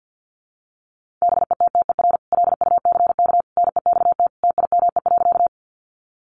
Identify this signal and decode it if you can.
Morse code, heard as a keyed tone, about 50 wpm, 700 Hz. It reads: BETTER LUCK NEXT TIME.
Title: Sentences from Top 1500 Words 50wpm